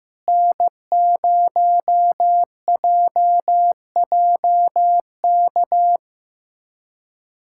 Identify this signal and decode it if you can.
Morse code, heard as a keyed tone, about 15 wpm, 700 Hz. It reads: N0JJK
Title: USA Callsign DE Callsign - 5 Characters - 15wpm